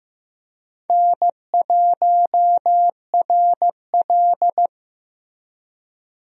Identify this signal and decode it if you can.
Morse code, heard as a keyed tone, about 15 wpm, 700 Hz. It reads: N1RL
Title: USA Call Signs - 4 Characters - Repeated 2x - 15wpm